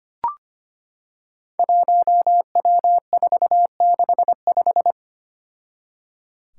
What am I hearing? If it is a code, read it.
Morse code, heard as a keyed tone, about 25 wpm, 700 Hz. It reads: 1W465
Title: Five Challenging Letter-Number Combinations 25wpm